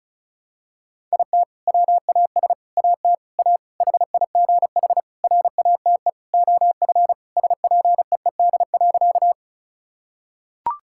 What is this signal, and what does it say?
Morse code, heard as a keyed tone, about 36 wpm, 700 Hz. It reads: IT WAS AT A HIGH RATE OF SPEED.
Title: Sentences from Top 1000 Words - Review Part 5 - 35wpm